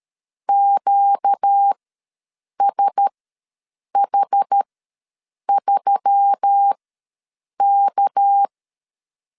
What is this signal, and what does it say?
Morse code, heard as a keyed tone, about 13 wpm, spaced as for 7 wpm, 800 Hz. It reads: QSH3K
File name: LEZIONI / 6 PKX0 / 6 PK ws 12wpm 3116